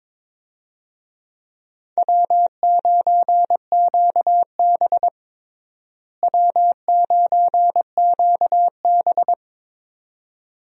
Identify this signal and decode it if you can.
Morse code, heard as a keyed tone, about 22 wpm, 700 Hz. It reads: W9QB W9QB
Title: USA Call Signs - 4 Characters - Repeated 2x - 22wpm